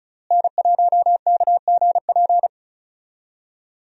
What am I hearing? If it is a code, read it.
Morse code, heard as a keyed tone, about 34 wpm, 700 Hz. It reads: N1KGP
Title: USA Call Signs - 5 Characters - Repeated 3x - 35wpm